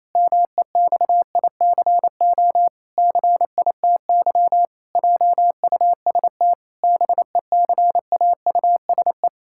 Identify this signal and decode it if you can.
Morse code, heard as a keyed tone, about 27 wpm, 700 Hz. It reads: MEXICO CITY JUST BECAUSE
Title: Pangram Challenge v2 - 28wpm